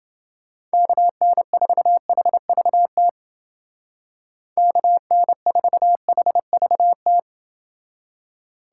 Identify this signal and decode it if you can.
Morse code, heard as a keyed tone, about 29 wpm, 700 Hz. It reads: KN4HVT KN4HVT